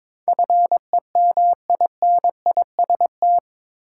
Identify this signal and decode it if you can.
Morse code, heard as a keyed tone, about 22 wpm, 700 Hz. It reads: FEMINIST
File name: practice / 22wpm / Eight Letter Words - 22wpm